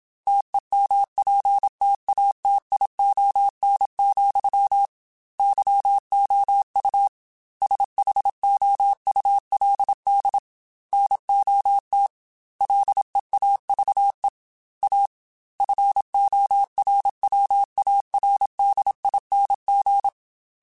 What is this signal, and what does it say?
Morse code, heard as a keyed tone, about 27 wpm, 800 Hz. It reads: TEMPTATION, YOU SHOULD NOT LEAVE A FORWARDING
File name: Chuckles / Chuckles-mp3s / Temptation @25